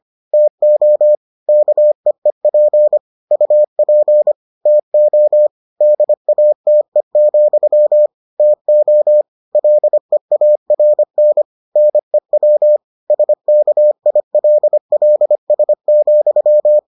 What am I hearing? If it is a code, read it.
Morse code, heard as a keyed tone, about 25 wpm, 600 Hz. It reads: TO KEEP UP TO DATE, TO LEARN NEW SKILLS,